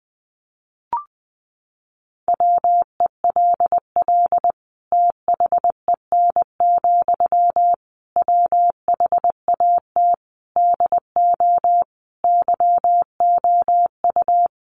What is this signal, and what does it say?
Morse code, heard as a keyed tone, about 20 wpm, 700 Hz. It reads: WELL THEN, WHAT DO YOU
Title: Sentences from Top 200 Words 20wpm